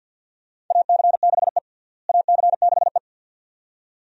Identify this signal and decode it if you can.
Morse code, heard as a keyed tone, about 48 wpm, 690 Hz. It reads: AC6E AC6E
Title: USA Call Signs - 4 Characters - Repeated 2x - 50wpm